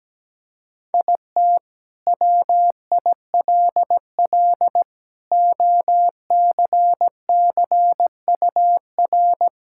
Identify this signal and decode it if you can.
Morse code, heard as a keyed tone, about 17 wpm, 700 Hz. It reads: IT WILL OCCUR
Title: Sentences from Top 1000 Words - Review Part 6 - 17wpm